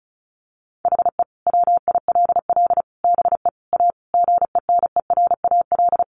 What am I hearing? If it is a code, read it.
Morse code, heard as a keyed tone, about 34 wpm, 700 Hz. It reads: HE WILL BE A GENERAL
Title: Sentences from Top 1200 Words 35wpm